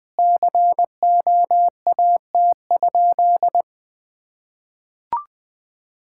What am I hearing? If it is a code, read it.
Morse code, heard as a keyed tone, about 20 wpm, 700 Hz. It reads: COAT?
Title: Sentences from Top 1600 Words 20wpm